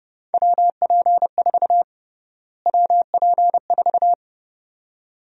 Morse code, heard as a keyed tone, about 31 wpm, 700 Hz. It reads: WP4 WP4
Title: USA Call Sign Prefixes - Repeated 2x - 30wpm